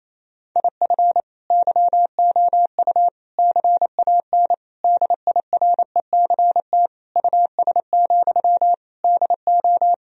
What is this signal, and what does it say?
Morse code, heard as a keyed tone, about 28 wpm, 700 Hz. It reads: IF YOU CAN DIRECT US, DO
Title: Sentences from Top 1000 Words - Review Part 2 - 28wpm